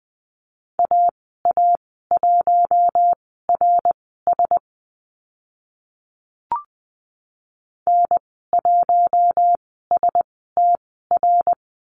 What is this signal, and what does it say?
Morse code, heard as a keyed tone, about 20 wpm, spaced as for 15 wpm, 700 Hz. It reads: AA1RS N1STR
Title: 082 Lesson 14g - Number 2 - Call Signs 20wpm